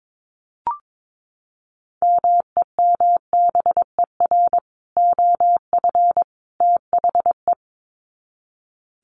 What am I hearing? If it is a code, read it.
Morse code, heard as a keyed tone, about 22 wpm, 700 Hz. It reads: MEMBER OF THE
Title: Sets of 3 Words 22wpm